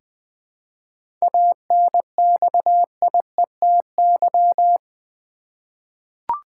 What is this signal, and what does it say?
Morse code, heard as a keyed tone, about 20 wpm, 700 Hz. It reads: ANXIETY
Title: Seven Letter Words - 20wpm